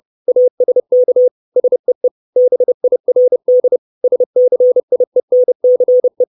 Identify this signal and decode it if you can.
Morse code, heard as a keyed tone, about 30 wpm, 500 Hz. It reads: ASK SEE BIRD SCIENCE
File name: 632 Words 30WPM